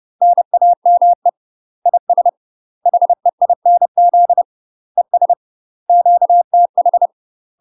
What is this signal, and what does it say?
Morse code, heard as a keyed tone, about 30 wpm, 700 Hz. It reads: NAME IS HEINZ ES QTH
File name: UK MASTER QSOs TRAINING/30WPM